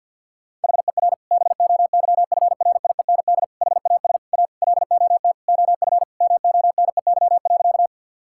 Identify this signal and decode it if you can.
Morse code, heard as a keyed tone, about 50 wpm, 700 Hz. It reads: HER BOYFRIEND HAS A LOT OF MONEY.
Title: Sentences from Top 1000 Words - Review Part 5 - 50wpm